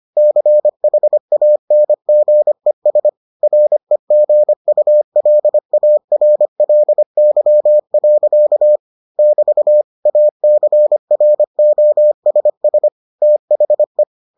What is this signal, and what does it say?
Morse code, heard as a keyed tone, about 25 wpm, 600 Hz. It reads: CHANGES REGULARLY. <BT> ACROSS THE